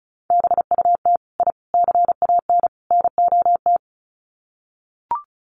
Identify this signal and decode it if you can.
Morse code, heard as a keyed tone, about 34 wpm, 700 Hz. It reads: BUT I CAN NOT
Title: Sets of 4 Words 35wpm